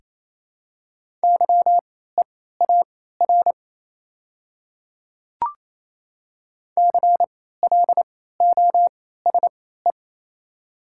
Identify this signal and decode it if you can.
Morse code, heard as a keyed tone, about 28 wpm, spaced as for 16 wpm, 700 Hz. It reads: YEAR CLOSE
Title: 3x Character Spacing - Top 500 Words - 28wpm